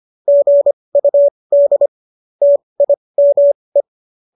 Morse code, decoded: GUD TIME

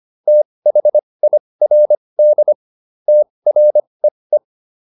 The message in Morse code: THIRD TREE

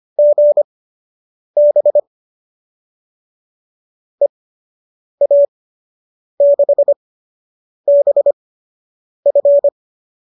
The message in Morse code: GB EA6BF

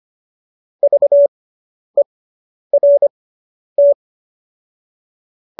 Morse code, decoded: VERT

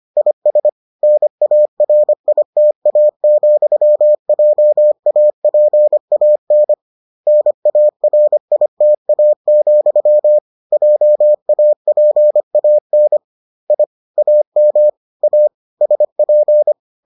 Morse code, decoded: IS NARITA, JAPAN NARITA, JAPAN I AM A SP

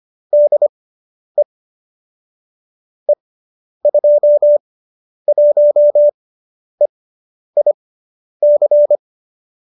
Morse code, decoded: DE E21EIC

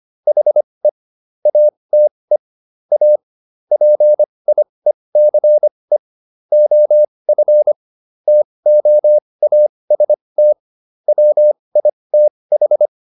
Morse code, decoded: HE ATE A PIECE OF TOAST WITH

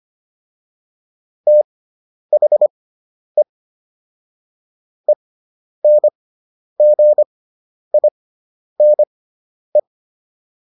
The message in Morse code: THE ENGINE

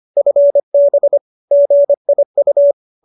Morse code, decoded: FB GIU

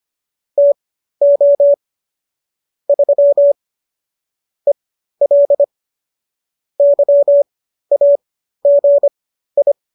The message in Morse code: TO 3 EL YAGI

